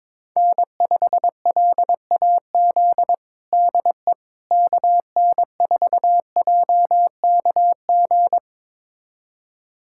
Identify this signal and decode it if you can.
Morse code, heard as a keyed tone, about 22 wpm, 700 Hz. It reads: N5LAZ DE KN4JKG
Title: USA Callsign DE Callsign 22wpm